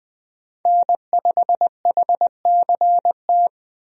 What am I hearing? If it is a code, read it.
Morse code, heard as a keyed tone, about 20 wpm, 700 Hz. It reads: N5HCT
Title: USA Call Signs - 5 Characters - Repeated 3x - 20wpm